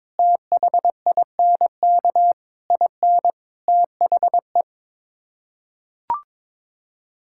Morse code, heard as a keyed tone, about 22 wpm, 700 Hz. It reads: THINK IN THE
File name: practice / 22wpm / Sets of 8 Words - 22wpm